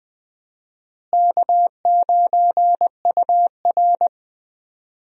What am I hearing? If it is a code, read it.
Morse code, heard as a keyed tone, about 20 wpm, 700 Hz. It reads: K9UR
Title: USA Call Signs - 4 Characters - Repeated 2x - 20wpm